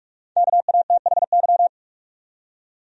Morse code, heard as a keyed tone, about 46 wpm, 700 Hz. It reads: KATHY
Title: USA Common Names - Max 5 Letters - 45wpm